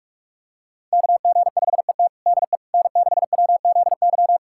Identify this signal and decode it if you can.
Morse code, heard as a keyed tone, about 43 wpm, 700 Hz. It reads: KG5ET DE N6WZY
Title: USA Callsign DE Callsign - 5 Characters - 45wpm